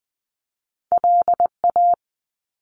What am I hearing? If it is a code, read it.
Morse code, heard as a keyed tone, about 20 wpm, 700 Hz. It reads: LA
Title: US States 20wpm